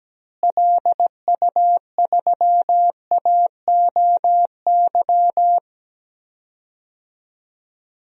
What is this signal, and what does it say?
Morse code, heard as a keyed tone, about 17 wpm, 700 Hz. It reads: LU3AOY
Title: DXCC-100 Call Signs - 6 Characters - Repeated 3x - 17wpm